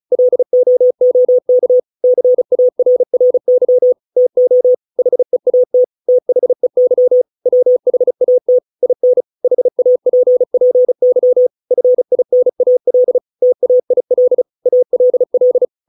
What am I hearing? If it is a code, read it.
Morse code, heard as a keyed tone, about 34 wpm, 500 Hz. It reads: LOOK CARRY TO HEAT THEY WHAT IN HAPPY FINAL TAIL ALL